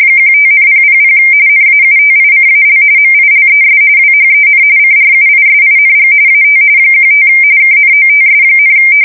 RTTY, about 45.45 baud. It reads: 15,000 NAUTICAL MILES WHEN CRUISING AT 15 KNOTS,
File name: fsk170 / USS-New-Jersey